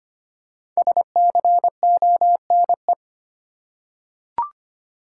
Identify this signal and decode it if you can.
Morse code, heard as a keyed tone, about 25 wpm, 700 Hz. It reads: SCONE
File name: practice / 25wpm / Difficult Words with Hints - 25wpm